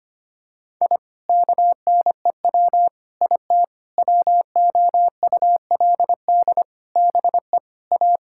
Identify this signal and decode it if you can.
Morse code, heard as a keyed tone, about 25 wpm, 700 Hz. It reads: I KNEW IT WOULD BE A
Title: Sentences from Top 1000 Words - Review Part 6 - 25wpm